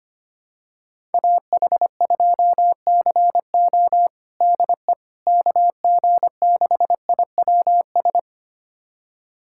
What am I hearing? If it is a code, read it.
Morse code, heard as a keyed tone, about 25 wpm, 700 Hz. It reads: AH2CO DE KG6IWS
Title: USA Callsign DE Callsign 25wpm